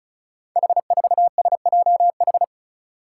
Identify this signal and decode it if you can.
Morse code, heard as a keyed tone, about 36 wpm, 700 Hz. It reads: H4SJH
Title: Five Challenging Letter-Number Combinations 35wpm